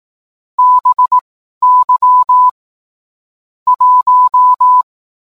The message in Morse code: BY 1

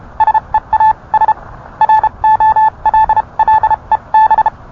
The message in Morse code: SEAS ROLLEB

